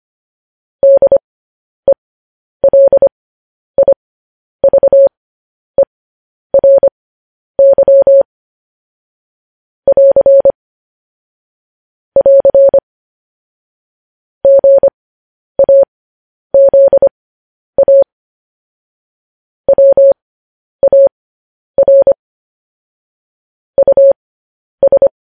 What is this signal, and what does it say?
Morse code, heard as a keyed tone, about 25 wpm, spaced as for 10 wpm, 550 Hz. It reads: DELIVERY <AR> <AR> GAZA WAR US